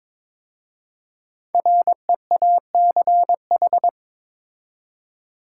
Morse code, heard as a keyed tone, about 22 wpm, 700 Hz. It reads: REACH
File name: 5 Letter Words - First 3 Letters Given - 22wpm